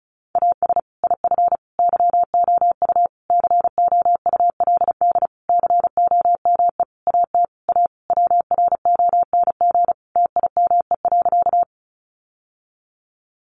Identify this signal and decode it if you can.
Morse code, heard as a keyed tone, about 34 wpm, 700 Hz. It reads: AS IF YOU COULD COME AT A WRONG TIME.
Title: Sentences from Top 900 Words 35wpm